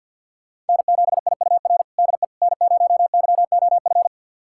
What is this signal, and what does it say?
Morse code, heard as a keyed tone, about 48 wpm, 700 Hz. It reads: N7IUR DE N0YOF
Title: USA Callsign DE Callsign - 5 Characters - 50wpm